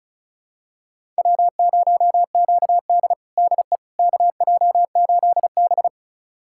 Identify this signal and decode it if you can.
Morse code, heard as a keyed tone, about 36 wpm, 700 Hz. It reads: W0QD DE KJ8B
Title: USA Callsign DE Callsign - 4 Characters - 35wpm